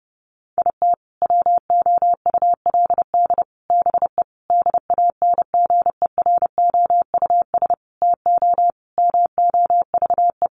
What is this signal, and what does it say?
Morse code, heard as a keyed tone, about 30 wpm, 700 Hz. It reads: IT WOULD BE DANGEROUS TO MOVE
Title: Sentences from Top 1200 Words 30wpm